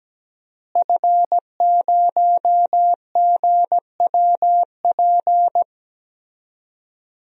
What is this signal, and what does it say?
Morse code, heard as a keyed tone, about 17 wpm, 700 Hz. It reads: F0GWP
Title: DXCC-100 Call Signs - 5 Characters - Repeated 2x - 17wpm